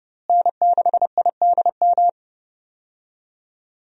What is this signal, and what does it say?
Morse code, heard as a keyed tone, about 32 wpm, 700 Hz. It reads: N6IDM
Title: USA Call Signs - 5 Characters - Repeated 3x - 30wpm